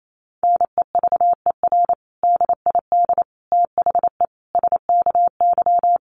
Morse code, heard as a keyed tone, about 28 wpm, 700 Hz. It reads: NEVER DID THE SKY
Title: Sentences from Top 1200 Words 28wpm